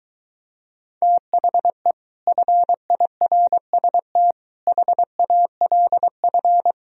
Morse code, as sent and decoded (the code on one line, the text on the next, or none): - .... / ..-. .. .-. ... - / .... .- .-.. ..-.
THE FIRST HALF